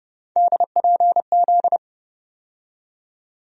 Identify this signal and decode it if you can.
Morse code, heard as a keyed tone, about 32 wpm, 700 Hz. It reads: DPZ
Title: USA Call Sign Postfixes - 3 Letters 30wpm